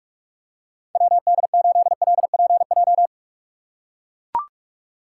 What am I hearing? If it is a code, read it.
Morse code, heard as a keyed tone, about 43 wpm, 700 Hz. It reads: WD8LPJ
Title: USA Call Signs - 6 Characters - 6x Character Spacing - 45wpm